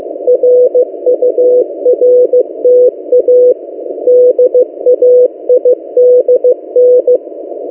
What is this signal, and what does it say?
Morse code, heard as a keyed tone, about 15 wpm, 500 Hz. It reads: RURTA DAIDN
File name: M08a- -1355utc-8096khz